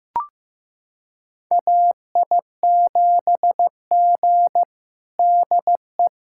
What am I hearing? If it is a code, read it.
Morse code, heard as a keyed tone, about 15 wpm, 700 Hz. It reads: AI7G DE
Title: USA Callsign DE Callsign - 4 Characters - 15wpm